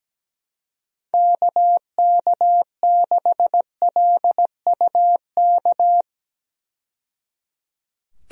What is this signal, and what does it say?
Morse code, heard as a keyed tone, about 17 wpm, 700 Hz. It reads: KK6LUK
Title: USA Call Signs - 6 Characters - 17wpm